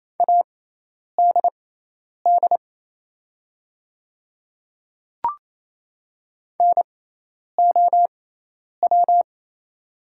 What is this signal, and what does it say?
Morse code, heard as a keyed tone, about 28 wpm, spaced as for 10 wpm, 690 Hz. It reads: ADD NOW
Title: 6x Character Spacing - Top 500 Words - 28wpm